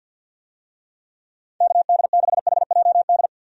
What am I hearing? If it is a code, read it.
Morse code, heard as a keyed tone, about 51 wpm, 700 Hz. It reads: KD6HJD
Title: USA Call Signs - 6 Characters - 50wpm